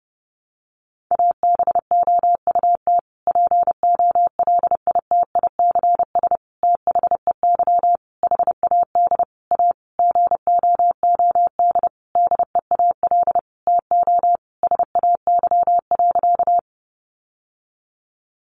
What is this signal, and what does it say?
Morse code, heard as a keyed tone, about 30 wpm, 700 Hz. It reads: ABOUT POLITICS THEY HAD A GOOD DEAL TO SAY.
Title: Sentences from Top 1000 Words 30wpm